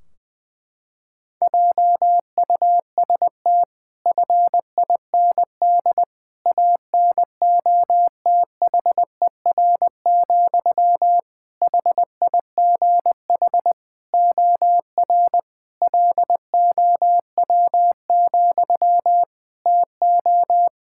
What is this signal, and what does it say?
Morse code, heard as a keyed tone, about 20 wpm, 700 Hz. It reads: JUST FIND ANOTHER, HIGH OR LOW, TO
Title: Sentences from Top 1600 Words 20wpm